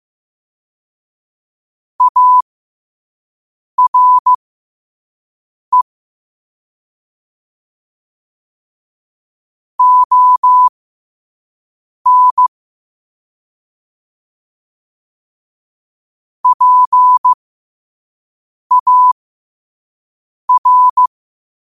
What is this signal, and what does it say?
Morse code, heard as a keyed tone, about 15 wpm, spaced as for 5 wpm, 1000 Hz. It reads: ARE ON PAR
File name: qotdfiles / 5wpm